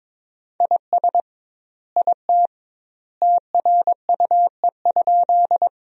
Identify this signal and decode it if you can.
Morse code, heard as a keyed tone, about 21 wpm, 700 Hz. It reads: IS IT TRUE?
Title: 2x Word-Spacing - Sentences from Top 400 Words - 22wpm